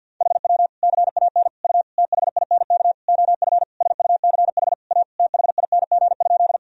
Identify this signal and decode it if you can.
Morse code, heard as a keyed tone, about 52 wpm, 700 Hz. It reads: HW CAN U THINK OF SUCH A THING?
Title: Sentences from Top 200 Words and Abbreviations 50wpm